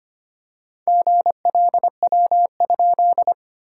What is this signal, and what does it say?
Morse code, heard as a keyed tone, about 25 wpm, 690 Hz. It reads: GLW?
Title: SOTA QSO Practice 25wpm